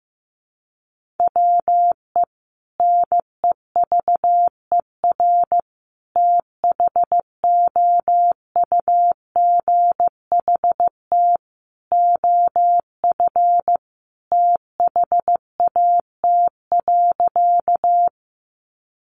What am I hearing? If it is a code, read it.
Morse code, heard as a keyed tone, about 15 wpm, 700 Hz. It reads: WE NEVER THOUGHT OF THAT.
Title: Sentences from Top 300 Words 15wpm